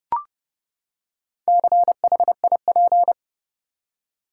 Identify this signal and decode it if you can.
Morse code, heard as a keyed tone, about 30 wpm, 700 Hz. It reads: CHIP
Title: Four Letter Words - 30wpm